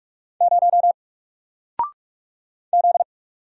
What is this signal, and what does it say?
Morse code, heard as a keyed tone, about 46 wpm, 700 Hz. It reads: 0 Z
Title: Single Letter-Number 45wpm